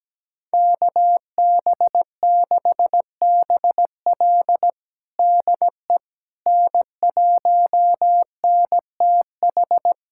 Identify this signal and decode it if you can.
Morse code, heard as a keyed tone, about 17 wpm, 700 Hz. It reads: KB6BL DE N1NTH
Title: USA Callsign DE Callsign - 5 Characters - 17wpm